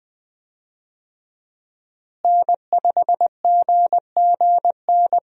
Morse code, heard as a keyed tone, about 20 wpm, 700 Hz. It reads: N5GGN
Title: USA Call Signs - 5 Characters - Repeated 3x - 20wpm